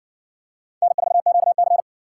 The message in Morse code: N4QC

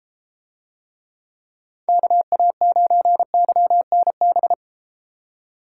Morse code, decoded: KA9YNB

